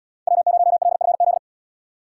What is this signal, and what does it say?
Morse code, heard as a keyed tone, about 77 wpm, 700 Hz. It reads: W0LFZ